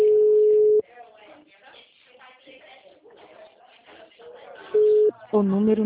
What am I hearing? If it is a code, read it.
Ringback tone (Europe, 425 Hz), with speech.